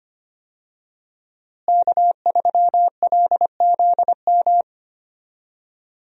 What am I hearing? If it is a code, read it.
Morse code, heard as a keyed tone, about 25 wpm, 700 Hz. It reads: K3LZM
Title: USA Call Signs - 5 Characters - Repeated 3x - 25wpm